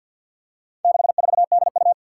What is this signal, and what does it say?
Morse code, heard as a keyed tone, about 52 wpm, 700 Hz. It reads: 64DU
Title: Four Challenging Letter-Number Combinations 50wpm